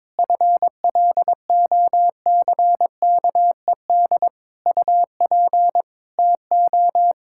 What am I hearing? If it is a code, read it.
Morse code, heard as a keyed tone, about 22 wpm, 700 Hz. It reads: FLOCKED UP TO
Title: Pangram Challenge v2 - 22wpm